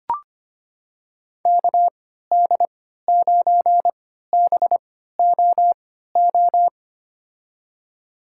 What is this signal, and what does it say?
Morse code, heard as a keyed tone, about 25 wpm, spaced as for 14 wpm, 700 Hz. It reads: KD9BOO
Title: USA Call Signs - 3x Character Spacing - 25wpm